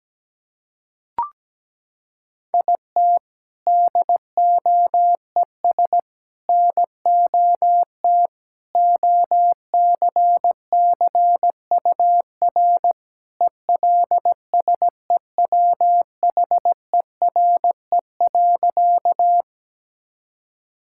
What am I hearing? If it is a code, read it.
Morse code, heard as a keyed tone, about 17 wpm, 700 Hz. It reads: IT DOES NOT OCCUR ELSEWHERE.
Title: Sentences from Top 1600 Words 17wpm